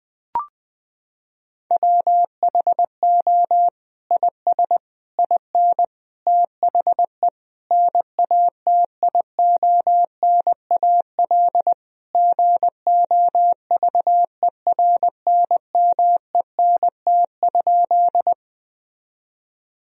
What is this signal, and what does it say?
Morse code, heard as a keyed tone, about 20 wpm, 700 Hz. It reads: WHO IS IN THE NATIONAL GOVERNMENT?
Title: Sentences from Top 1000 Words - Review Part 4 - 20wpm